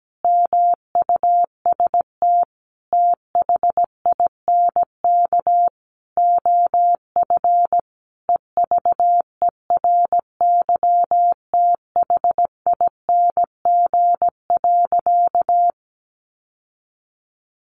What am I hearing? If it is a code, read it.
Morse code, heard as a keyed tone, about 17 wpm, 700 Hz. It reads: MUST THINK OF EVERYTHING.
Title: Sentences from Top 1000 Words 17wpm